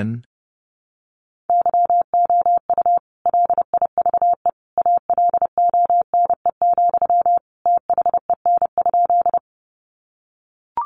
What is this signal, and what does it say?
Morse code, heard as a keyed tone, about 30 wpm, 700 Hz. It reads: YOU LIVE ALONE, THEN?
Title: Sentences from Top 1000 Words 30wpm